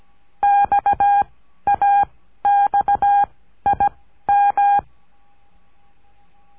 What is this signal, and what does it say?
Morse code, heard as a keyed tone, about 17 wpm, 810 Hz. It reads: X A X I M
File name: cursos / CURSO DE TELEGRAFIA / Telegrafia - Trilha 14